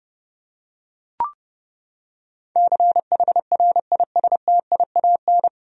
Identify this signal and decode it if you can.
Morse code, heard as a keyed tone, about 30 wpm, 700 Hz. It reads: CHRISTIAN
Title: USA Common Names - 30wpm